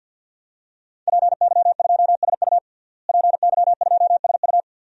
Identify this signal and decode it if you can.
Morse code, heard as a keyed tone, about 45 wpm, 700 Hz. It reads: PY2SU PY2SU
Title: DXCC-100 Call Signs - 5 Characters - Repeated 2x - 50wpm